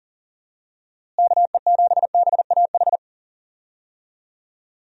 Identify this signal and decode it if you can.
Morse code, heard as a keyed tone, about 40 wpm, 700 Hz. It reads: KE7BAH
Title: USA Call Signs - 6 Characters - Repeated 2x - 40wpm